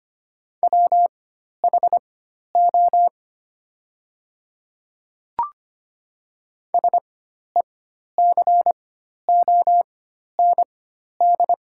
Morse code, heard as a keyed tone, about 25 wpm, spaced as for 12 wpm, 690 Hz. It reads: WHO SECOND
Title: 4x Character Spacing - Top 300 Words - 25wpm